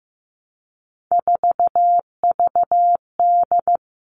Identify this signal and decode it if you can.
Morse code, heard as a keyed tone, about 15 wpm, 700 Hz. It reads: 4VD
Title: All Cut Numbers - 3-Digit 15wpm